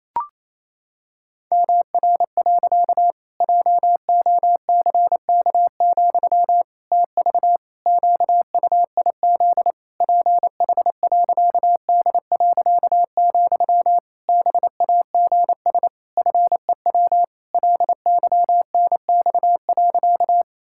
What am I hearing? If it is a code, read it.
Morse code, heard as a keyed tone, about 28 wpm, 700 Hz. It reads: MR. JOCK, TV QUIZ PH.D., BAGS FEW LYNX.
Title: Pangram Challenge v2 - 28wpm